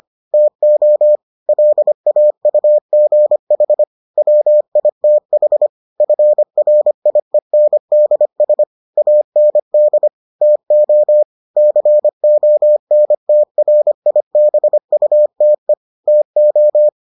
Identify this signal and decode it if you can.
Morse code, heard as a keyed tone, about 25 wpm, 600 Hz. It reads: TO LAUGH WITH FRIENDS AND TO CONTRIBUTE TO